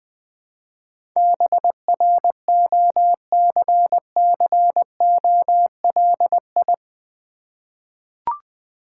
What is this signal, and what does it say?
Morse code, heard as a keyed tone, about 20 wpm, 700 Hz. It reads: BROCCOLI
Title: Difficult Words with Hints - 20wpm